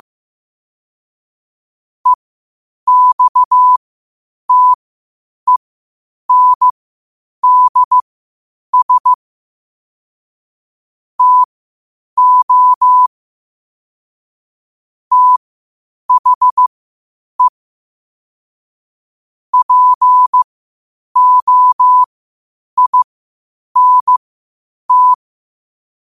Morse code, heard as a keyed tone, about 15 wpm, spaced as for 8 wpm, 1000 Hz. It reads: EXTENDS TO THE POINT
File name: qotdfiles / 8wpm